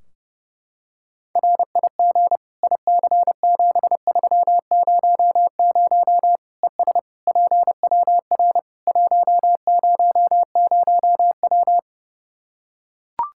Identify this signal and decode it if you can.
Morse code, heard as a keyed tone, about 30 wpm, 700 Hz. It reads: RIG IC7300 ES PWR 100W
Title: Practice QSO - Exchange 2 - Info - 30wpm